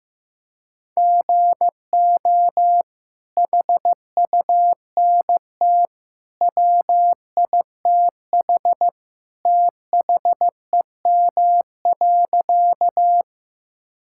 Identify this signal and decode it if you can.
Morse code, heard as a keyed tone, about 15 wpm, 700 Hz. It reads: GO HUNT WITH THEM.
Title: Sentences from Top 1600 Words 15wpm